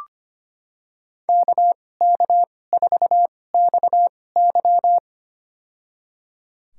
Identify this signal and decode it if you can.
Morse code, heard as a keyed tone, about 25 wpm, spaced as for 18 wpm, 700 Hz. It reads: KK4XY
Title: USA Call Signs - 2x Character Spacing - 25wpm